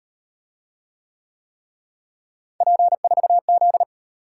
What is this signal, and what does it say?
Morse code, heard as a keyed tone, about 34 wpm, 700 Hz. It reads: P4Z